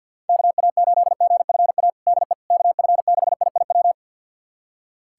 Morse code, heard as a keyed tone, about 52 wpm, 700 Hz. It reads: KA8GFA DE KF6IIW
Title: USA Callsign DE Callsign 50wpm